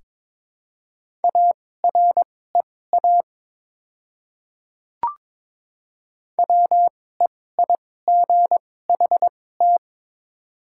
Morse code, heard as a keyed tone, about 22 wpm, spaced as for 16 wpm, 700 Hz. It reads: AREA WEIGHT
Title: 2x Character Spacing - Top 500 Words - 22wpm